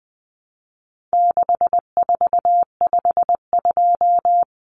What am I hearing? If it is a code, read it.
Morse code, heard as a keyed tone, about 20 wpm, 700 Hz. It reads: 6452
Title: Common Cut Numbers - 4-Digit 20wpm